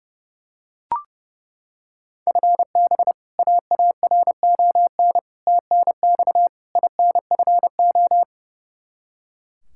Morse code, heard as a keyed tone, about 32 wpm, 700 Hz. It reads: FB AARON TNX INFO
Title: K1USN SST QSO Practice 30wpm